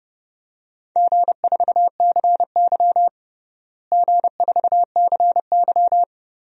Morse code, heard as a keyed tone, about 30 wpm, 700 Hz. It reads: G4CY G4CY
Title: DXCC-100 Call Signs - 4 Characters - Repeated 2x - 30wpm